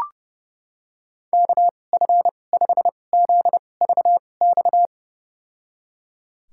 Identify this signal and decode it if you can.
Morse code, heard as a keyed tone, about 30 wpm, spaced as for 22 wpm, 700 Hz. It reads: KF5ZVX
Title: USA Call Signs - 6 Characters - 2x Character Spacing - 30wpm